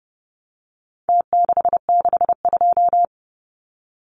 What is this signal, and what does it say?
Morse code, heard as a keyed tone, about 30 wpm, 700 Hz. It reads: T662
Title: Numbers - Contest Style - 4-Digit 30wpm